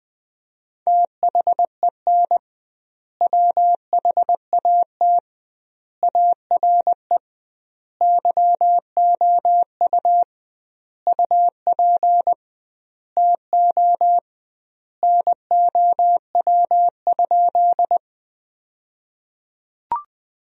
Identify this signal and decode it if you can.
Morse code, heard as a keyed tone, about 20 wpm, 700 Hz. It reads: THEN WHAT ARE YOU UP TO NOW?
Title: 2x Word-Spacing - Sentences from Top 100 Words - 20wpm